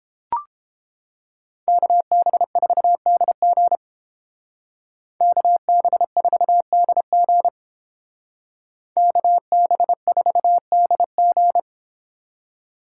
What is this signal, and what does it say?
Morse code, heard as a keyed tone, about 29 wpm, 700 Hz. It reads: KB4DG KB4DG KB4DG